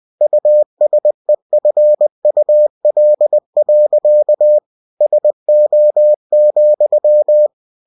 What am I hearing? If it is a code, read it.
Morse code, heard as a keyed tone, about 20 wpm, 600 Hz. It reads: USEFUL. SO,
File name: cw-books / winnie-the-pooh / winnie-20wpm-page-0050